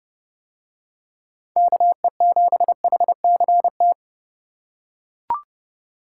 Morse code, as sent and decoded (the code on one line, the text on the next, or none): -.- . --... .... -.-. -
KE7HCT